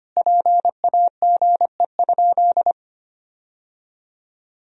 Morse code, heard as a keyed tone, about 25 wpm, 700 Hz. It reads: PAGE?
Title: Sentences from Top 1000 Words - Review Part 1 - 25wpm